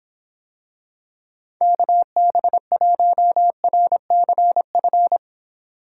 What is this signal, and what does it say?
Morse code, heard as a keyed tone, about 24 wpm, 700 Hz. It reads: KB1RCF